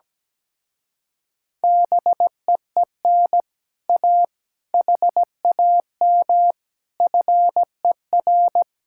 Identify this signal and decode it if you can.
Morse code, heard as a keyed tone, about 17 wpm, 700 Hz. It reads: BEEN A HAM FER
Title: Speed-Racing Common QSO Phrases Encore - 17wpm